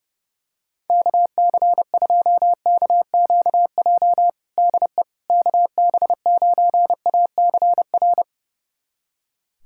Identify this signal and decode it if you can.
Morse code, heard as a keyed tone, about 30 wpm, 700 Hz. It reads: KC2KQJ DE KB9ACR
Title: USA Callsign DE Callsign - 6 Characters - 30wpm